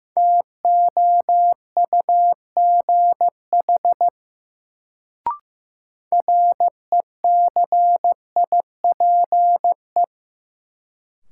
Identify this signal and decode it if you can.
Morse code, heard as a keyed tone, about 15 wpm, 700 Hz. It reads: TOUGH RECIPE
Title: Sentences from Top 1200 Words - New Word Drill - 15wpm